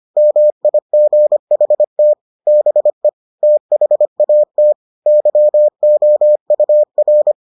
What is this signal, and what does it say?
Morse code, heard as a keyed tone, about 25 wpm, 600 Hz. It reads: MIGHT BE THAT YOUR